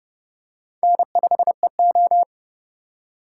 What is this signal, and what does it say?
Morse code, heard as a keyed tone, about 30 wpm, 700 Hz. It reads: N5EO